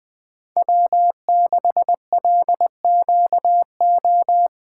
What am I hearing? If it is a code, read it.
Morse code, heard as a keyed tone, about 20 wpm, 700 Hz. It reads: W6LQO